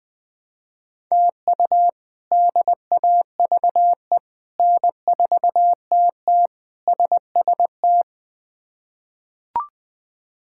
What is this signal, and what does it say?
Morse code, heard as a keyed tone, about 20 wpm, 700 Hz. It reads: TU DAVE N4TT SST
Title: K1USN SST QSO Practice - 20wpm